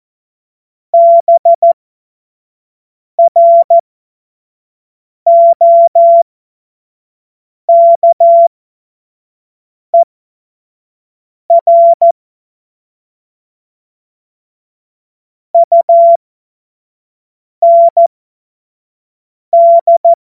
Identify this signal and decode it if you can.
Morse code, heard as a keyed tone, about 14 wpm, spaced as for 5 wpm, 680 Hz. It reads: BROKER UND
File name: Morsekurs / Witze / Witz 17 Juni 2025 25 F70